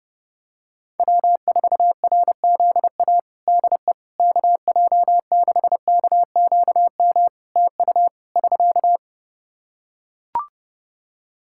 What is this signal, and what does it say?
Morse code, heard as a keyed tone, about 32 wpm, 700 Hz. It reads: W4RZA DE KJ6KQM TU <SK>
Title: Practice QSO - Exchange 4 - Ending - 30wpm